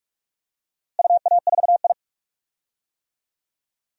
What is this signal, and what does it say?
Morse code, heard as a keyed tone, about 46 wpm, 700 Hz. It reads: UA4I
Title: DXCC-100 Call Signs - 4 Characters - Repeated 3x - 45wpm